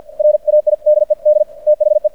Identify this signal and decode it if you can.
Morse code, heard as a keyed tone, about 50 wpm, 600 Hz. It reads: RAIDED THE
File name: archived / CRSnet